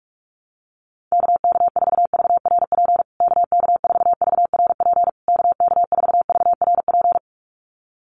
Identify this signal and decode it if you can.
Morse code, heard as a keyed tone, about 44 wpm, 700 Hz. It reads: KK4VRP KK4VRP KK4VRP
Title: USA Call Signs Repeated 3x 45wpm